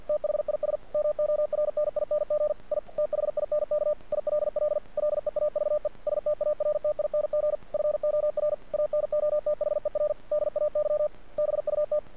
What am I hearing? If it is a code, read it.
Morse code, heard as a keyed tone, about 48 wpm, 600 Hz. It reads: THIS MORNING I THINK ILL LEAVE STARTING FOR ANOTHER DAY BUT
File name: archived / CRSnet